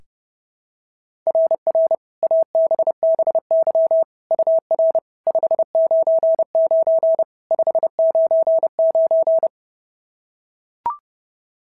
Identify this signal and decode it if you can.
Morse code, heard as a keyed tone, about 30 wpm, 650 Hz. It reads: RR ABBY UR 599 599
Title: SOTA QSO Practice 30wpm